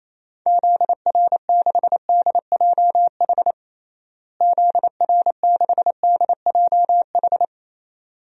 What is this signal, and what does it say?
Morse code, heard as a keyed tone, about 27 wpm, 700 Hz. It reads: ZR6DJH ZR6DJH
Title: DXCC-100 Call Signs - 6 Characters - Repeated 2x - 28wpm